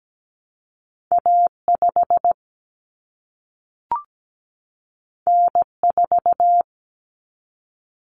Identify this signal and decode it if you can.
Morse code, heard as a keyed tone, about 17 wpm, 700 Hz. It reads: A5 N4
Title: Common Cut Numbers - 2-Digit 17wpm